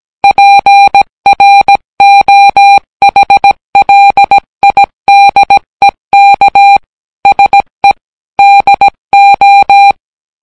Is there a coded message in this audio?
Yes: Morse code from 0.2 to 9.9 s, heard as a keyed tone, about 17 wpm, 800 Hz. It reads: PROHLIDEK SE DO